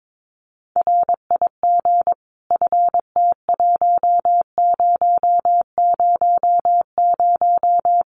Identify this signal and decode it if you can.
Morse code, heard as a keyed tone, about 22 wpm, 700 Hz. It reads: RIG FT1000
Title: Common QSO Phrases 22wpm